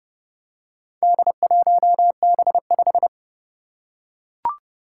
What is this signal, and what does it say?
Morse code, heard as a keyed tone, about 30 wpm, 700 Hz. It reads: D1B5
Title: Four Challenging Letter-Number Combinations 30wpm